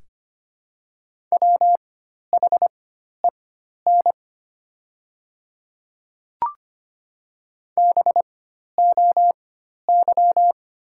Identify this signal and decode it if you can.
Morse code, heard as a keyed tone, about 25 wpm, spaced as for 12 wpm, 690 Hz. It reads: WHEN BOY
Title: 4x Character Spacing - Top 300 Words - 25wpm